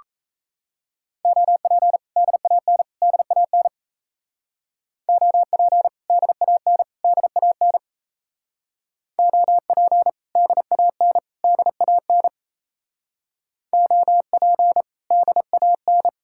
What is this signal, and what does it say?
Morse code, heard as a keyed tone, about 34 wpm, 700 Hz. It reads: OP DAN DAN OP DAN DAN OP DAN DAN OP DAN
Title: Speed-Racing Common QSO Phrases Encore - 28wpm